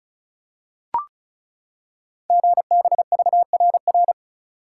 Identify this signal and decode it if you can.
Morse code, heard as a keyed tone, about 35 wpm, 690 Hz. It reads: GBVRR
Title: Code Groups - 5 Letters - 35wpm